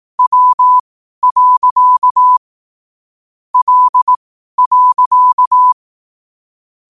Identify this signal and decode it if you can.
Morse code, heard as a keyed tone, about 18 wpm, spaced as for 12 wpm, 1000 Hz. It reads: W. L.
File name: qotdfiles / 12wpm